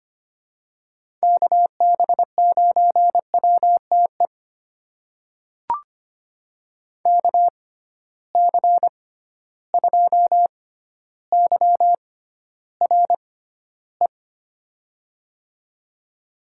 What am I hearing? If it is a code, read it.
Morse code, heard as a keyed tone, about 25 wpm, spaced as for 13 wpm, 700 Hz. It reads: KB9WTE KC2YRE
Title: USA Call Signs - 6 Characters - 6x Character Spacing - 25wpm